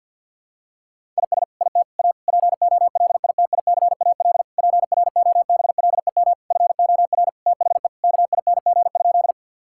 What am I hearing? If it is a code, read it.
Morse code, heard as a keyed tone, about 46 wpm, 700 Hz. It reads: IS IT A POLITICAL PROBLEM FOR THE KING?
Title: Sentences from Top 1000 Words - Review Part 5 - 50wpm